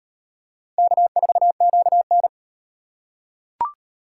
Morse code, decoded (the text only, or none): K4QN